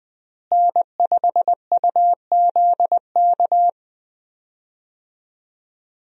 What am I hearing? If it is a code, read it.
Morse code, heard as a keyed tone, about 20 wpm, 700 Hz. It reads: N5UZK
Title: USA Call Signs - 5 Characters - Repeated 2x - 20wpm